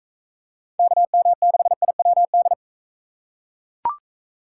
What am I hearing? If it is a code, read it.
Morse code, heard as a keyed tone, about 41 wpm, 700 Hz. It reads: KM6IWD